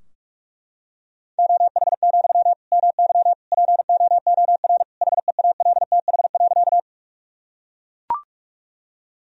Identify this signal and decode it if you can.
Morse code, heard as a keyed tone, about 41 wpm, 700 Hz. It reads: OH, MY POOR HEALTH.
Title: Sentences from Top 1000 Words - Review Part 3 - 45wpm